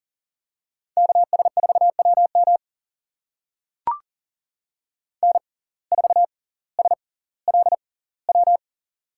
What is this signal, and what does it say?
Morse code, heard as a keyed tone, about 40 wpm, spaced as for 20 wpm, 700 Hz. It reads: KS4WM N4SLW
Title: USA Call Signs - 6x Character Spacing - 40wpm